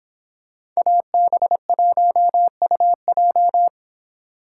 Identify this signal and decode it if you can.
Morse code, heard as a keyed tone, about 26 wpm, 700 Hz. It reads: AB1UJ